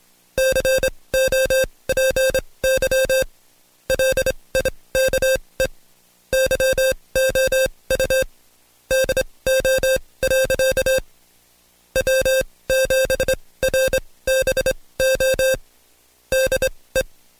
Morse code, heard as a keyed tone, about 25 wpm, spaced as for 19 wpm, 530 Hz. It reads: COPY LIKE YOU DO. W7RBO DE
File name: Code Quick / AudioDownld2 / Seg136-Stepper9 19WPM